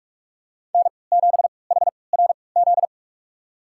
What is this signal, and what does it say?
Morse code, heard as a keyed tone, about 43 wpm, spaced as for 25 wpm, 700 Hz. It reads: N7HRZ
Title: USA Call Signs - 5 Characters - 3x Character Spacing - 45wpm